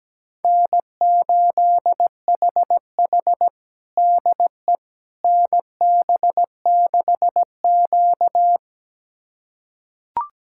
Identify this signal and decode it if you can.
Morse code, heard as a keyed tone, about 17 wpm, 700 Hz. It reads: N8HH DE NB6Q
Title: USA Callsign DE Callsign - 4 Characters - 17wpm